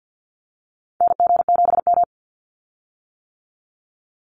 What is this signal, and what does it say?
Morse code, heard as a keyed tone, about 50 wpm, 700 Hz. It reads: NG7M
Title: Call Signs - 50wpm